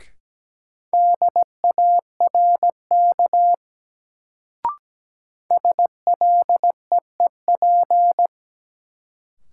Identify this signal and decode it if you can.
Morse code, heard as a keyed tone, about 17 wpm, 700 Hz. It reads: DARK SLEEP
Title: Sentences from Top 500 Words - New Word Drill - 17wpm